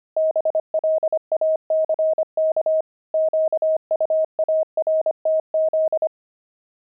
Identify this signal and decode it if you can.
Morse code, heard as a keyed tone, about 25 wpm, 630 Hz. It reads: BLACK QUARTZ
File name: pangram-fs-25-cs-25-love